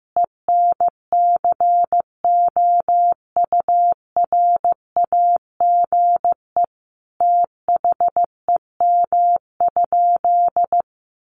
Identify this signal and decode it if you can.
Morse code, heard as a keyed tone, about 15 wpm, 700 Hz. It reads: ENCOURAGE THEM?
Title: Sentences from Top 1400 Words 15wpm